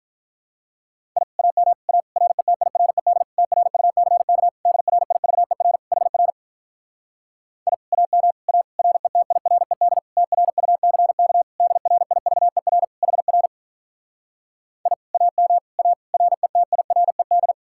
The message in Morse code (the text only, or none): I AM A RETIRED TRUCK DRIVER HR I AM A RETIRED TRUCK DRIVER HR I AM A RETIRED